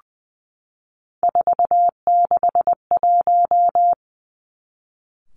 Morse code, heard as a keyed tone, about 20 wpm, 700 Hz. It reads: 461
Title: Three Challenging Letter-Number Combinations 20wpm